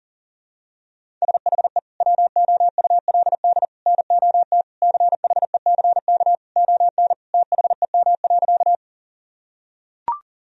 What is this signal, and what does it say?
Morse code, heard as a keyed tone, about 40 wpm, 700 Hz. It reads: SHE WOULD NOT CHECK ON THEM.
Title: Sentences from Top 1000 Words - Review Part 3 - 40wpm